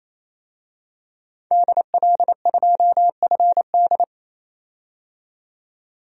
Morse code, heard as a keyed tone, about 28 wpm, 700 Hz. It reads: DL2FD
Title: DXCC-100 Call Signs - 5 Characters - 28wpm